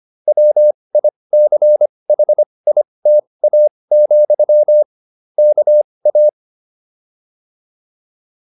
Morse code, heard as a keyed tone, about 25 wpm, spaced as for 20 wpm, 600 Hz. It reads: WICHITA, KA
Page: intermediate-practice-files